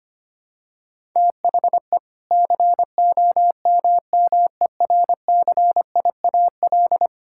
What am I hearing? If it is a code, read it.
Morse code, heard as a keyed tone, about 25 wpm, 700 Hz. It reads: THE COMMERCIAL
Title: Sentences from Top 1000 Words - Review Part 5 - 25wpm